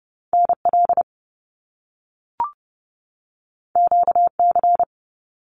Morse code, heard as a keyed tone, about 30 wpm, 700 Hz. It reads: NL QC
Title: Canadian Provinces and Territories 30wpm